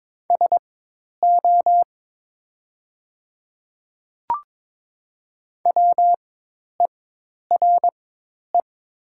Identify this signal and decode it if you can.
Morse code, heard as a keyed tone, about 22 wpm, spaced as for 10 wpm, 700 Hz. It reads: SO WERE
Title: 4x Character Spacing - Top 100 Words - 22wpm